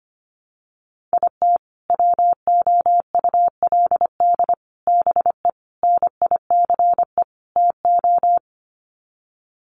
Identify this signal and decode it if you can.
Morse code, heard as a keyed tone, about 25 wpm, 700 Hz. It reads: IT WOULD BE NICE TO
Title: Sets of 5 Words Encore2 25wpm